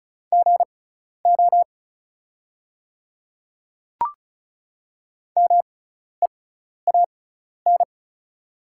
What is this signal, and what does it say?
Morse code, heard as a keyed tone, about 34 wpm, spaced as for 12 wpm, 690 Hz. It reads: GO MEAN
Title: 6x Character Spacing - Top 500 Words - 35wpm